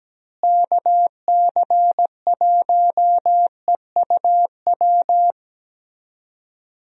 Morse code, heard as a keyed tone, about 17 wpm, 700 Hz. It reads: KC1EUW
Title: USA Call Signs - 6 Characters - Repeated 3x - 17wpm